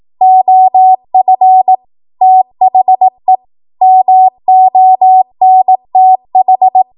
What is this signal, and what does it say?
Morse code, heard as a keyed tone, about 18 wpm, 750 Hz. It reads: OF THE MONTH
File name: Morse / 18WPM